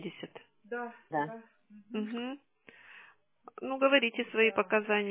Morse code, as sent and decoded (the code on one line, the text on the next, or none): none